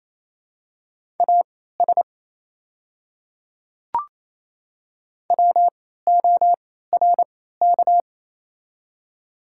Morse code, heard as a keyed tone, about 28 wpm, spaced as for 16 wpm, 700 Hz. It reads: AS WORK
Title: 3x Character Spacing - Top 200 Words - 28wpm